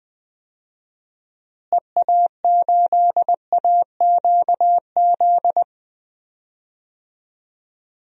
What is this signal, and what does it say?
Morse code, heard as a keyed tone, about 20 wpm, 700 Hz. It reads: EA8AQZ
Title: DXCC-100 Call Signs - 6 Characters - Repeated 2x - 20wpm